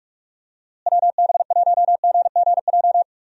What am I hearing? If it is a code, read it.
Morse code, heard as a keyed tone, about 46 wpm, 700 Hz. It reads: WB1GGJ
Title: USA Call Signs - 6 Characters - Repeated 2x - 45wpm